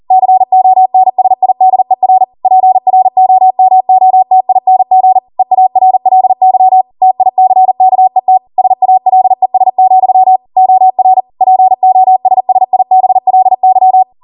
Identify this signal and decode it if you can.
Morse code, heard as a keyed tone, about 40 wpm, 750 Hz. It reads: CONSIDER PROMOTING EARLY TICKET SALES, OR POSSIBLY